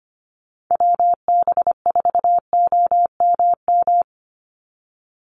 Morse code, heard as a keyed tone, about 25 wpm, 700 Hz. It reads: WB4OMM
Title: Call Signs Repeated 2x - 25wpm